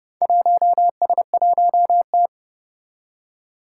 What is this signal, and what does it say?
Morse code, heard as a keyed tone, about 30 wpm, 700 Hz. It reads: 1S1T
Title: Four Challenging Letter-Number Combinations 30wpm